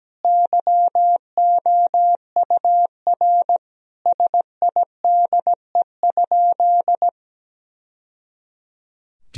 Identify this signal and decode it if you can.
Morse code, heard as a keyed tone, about 17 wpm, 700 Hz. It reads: YOUR SIDE?
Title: Sentences from Top 1000 Words - Review Part 3 - 17wpm